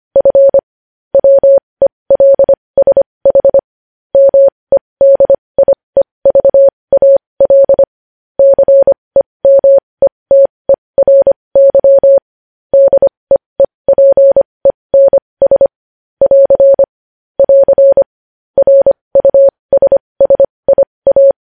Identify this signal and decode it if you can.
Morse code, heard as a keyed tone, about 25 wpm, spaced as for 20 wpm, 550 Hz. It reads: F WELSH MEDIEVAL CEMETERY DEEPENS <AR> <AR> RUSSIA
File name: podcast / News headlines-1x-25x20